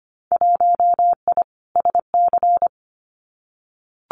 Morse code, heard as a keyed tone, about 25 wpm, 700 Hz. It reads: 1I SC
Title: Winter Field Day Exchanges 25wpm